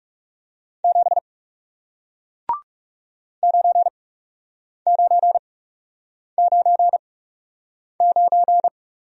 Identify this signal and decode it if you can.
Morse code, heard as a keyed tone, about 40 wpm, spaced as for 8 wpm, 700 Hz. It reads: 7 9999